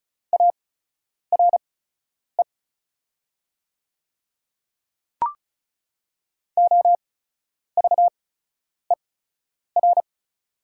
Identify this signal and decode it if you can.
Morse code, heard as a keyed tone, about 36 wpm, spaced as for 10 wpm, 700 Hz. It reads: ARE OVER
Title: 8x Character Spacing - Top 100 Words - 35wpm